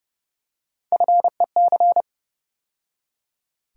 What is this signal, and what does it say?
Morse code, heard as a keyed tone, about 30 wpm, 700 Hz. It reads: FEC
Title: USA Call Sign Postfixes - 3 Letters 30wpm